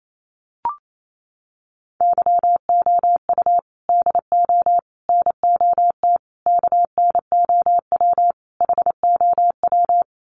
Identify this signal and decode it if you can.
Morse code, heard as a keyed tone, about 28 wpm, 700 Hz. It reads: YOU DO NOT KNOW HOW
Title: Sets of 5 Words Encore 28wpm